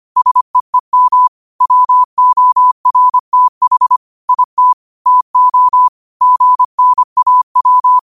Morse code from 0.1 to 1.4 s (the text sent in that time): SEEM